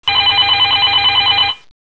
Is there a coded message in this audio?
No.